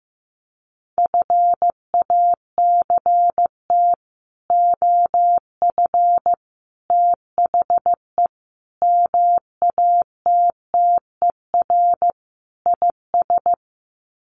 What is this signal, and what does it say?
Morse code, heard as a keyed tone, about 15 wpm, 700 Hz. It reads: FACT OF THE MATTER IS
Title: Sets of 5 Words 15wpm